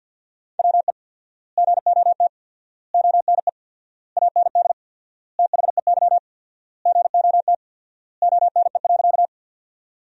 Morse code, heard as a keyed tone, about 53 wpm, 700 Hz. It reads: WE GOT ONE AND THEY GOT ONE.
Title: 4x Word-Spacing - Sentences from Top 300 Words - 50wpm